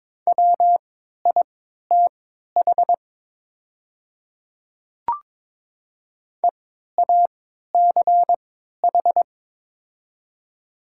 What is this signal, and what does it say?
Morse code, heard as a keyed tone, about 22 wpm, spaced as for 13 wpm, 700 Hz. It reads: WITH EACH
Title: 3x Character Spacing - Top 100 Words - 22wpm